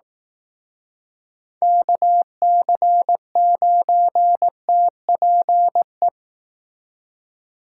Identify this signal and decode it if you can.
Morse code, heard as a keyed tone, about 18 wpm, 700 Hz. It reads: KC9TPE
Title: Speed Racing - USA Call Signs 15wpm